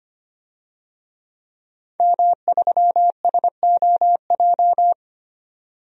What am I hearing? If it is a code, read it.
Morse code, heard as a keyed tone, about 25 wpm, 700 Hz. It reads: M3SOJ
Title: DXCC-100 Call Signs - 5 Characters - 25wpm